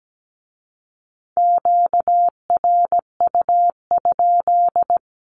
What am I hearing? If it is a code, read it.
Morse code, heard as a keyed tone, about 17 wpm, 700 Hz. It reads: QRU?
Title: Q-Codes with Spoken Meaning - 17wpm